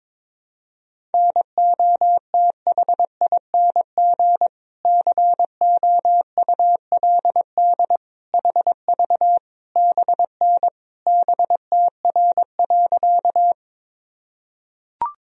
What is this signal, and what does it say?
Morse code, heard as a keyed tone, about 22 wpm, 700 Hz. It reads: NOTHING COULD HV BN BTR.
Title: Sentences from Top 500 Words and Abbreviations 22wpm